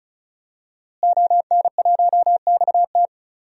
Morse code, heard as a keyed tone, about 34 wpm, 700 Hz. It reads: ON1XT
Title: DXCC-100 Call Signs - 5 Characters - Repeated 2x - 35wpm